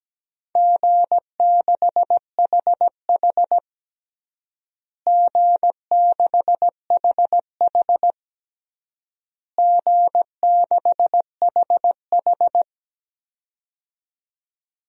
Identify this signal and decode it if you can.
Morse code, heard as a keyed tone, about 17 wpm, 700 Hz. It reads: G6HH G6HH G6HH
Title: DXCC-100 Call Signs - 4 Characters - Repeated 3x - 17wpm